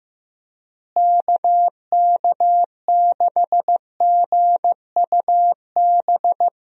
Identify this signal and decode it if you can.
Morse code, heard as a keyed tone, about 15 wpm, 700 Hz. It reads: KK6GUB
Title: USA Call Signs - 6 Characters - Repeated 3x - 15wpm